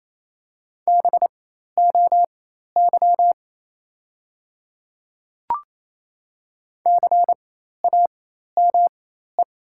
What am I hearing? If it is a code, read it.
Morse code, heard as a keyed tone, about 28 wpm, spaced as for 13 wpm, 700 Hz. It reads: BOY CAME
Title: 4x Character Spacing - Top 200 Words - 28wpm